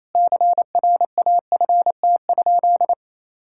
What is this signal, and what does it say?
Morse code, heard as a keyed tone, about 27 wpm, 700 Hz. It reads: CRAFT?
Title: Sentences from Top 1000 Words - Review Part 6 - 28wpm